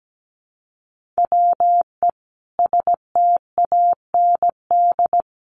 Morse code, heard as a keyed tone, about 17 wpm, 700 Hz. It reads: WE STAND
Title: Sentences from Top 1200 Words 17wpm